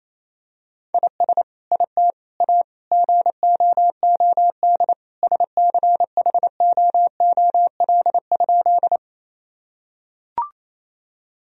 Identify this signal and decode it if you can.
Morse code, heard as a keyed tone, about 27 wpm, 700 Hz. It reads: IS IT A GOOD SCHOOL?
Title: Sentences from Top 1000 Words - Review Part 1 - 28wpm